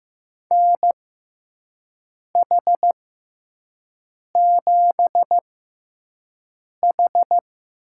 Morse code, heard as a keyed tone, about 15 wpm, spaced as for 5 wpm, 700 Hz. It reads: NH7H
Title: USA Call Signs - 6x Character Spacing - 15wpm